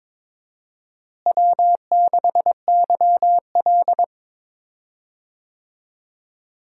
Morse code, heard as a keyed tone, about 22 wpm, 700 Hz. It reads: W6YL
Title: USA Call Signs - 4 Characters - Repeated 3x - 22wpm